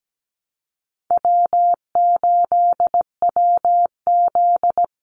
Morse code, heard as a keyed tone, about 17 wpm, 700 Hz. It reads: W8WZ
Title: Call Signs - 17wpm